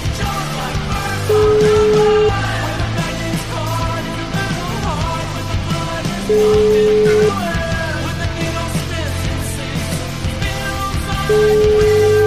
Ringback tone (Europe, 425 Hz), with noise.